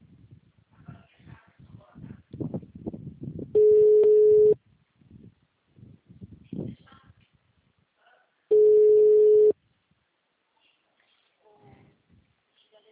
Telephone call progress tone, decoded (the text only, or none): ringback tone